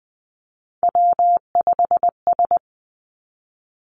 Morse code, heard as a keyed tone, about 20 wpm, 700 Hz. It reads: W5S
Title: Three Challenging Letter-Number Combinations 20wpm